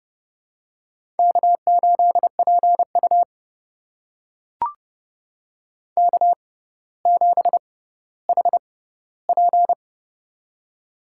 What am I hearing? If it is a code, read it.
Morse code, heard as a keyed tone, about 29 wpm, spaced as for 15 wpm, 700 Hz. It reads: K8PU K7HP
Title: USA Call Signs - 4 Characters - 6x Character Spacing - 30wpm